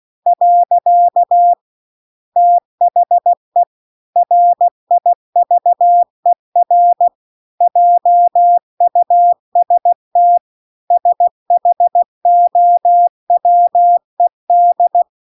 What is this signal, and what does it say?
Morse code, heard as a keyed tone, about 16 wpm, 700 Hz. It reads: . THE RIVER JUST SHOWED